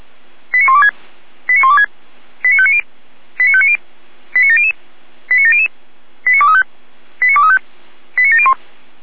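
Five-tone selective call (ZVEI), heard as quick runs of five tones, tones of about 70 ms. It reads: 89126 89126 89590 89590 89700 89700 89235 89235 89701
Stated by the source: ZVEI five-tone call received on SDR.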